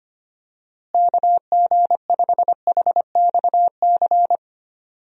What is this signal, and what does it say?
Morse code, heard as a keyed tone, about 25 wpm, 700 Hz. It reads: KG5HXC
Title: USA Call Signs - 6 Characters - Repeated 3x - 25wpm